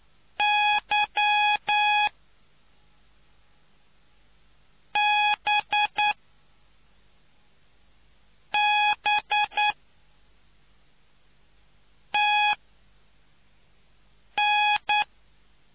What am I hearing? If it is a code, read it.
Morse code, heard as a keyed tone, about 9 wpm, spaced as for 4 wpm, 840 Hz. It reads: Y B B TN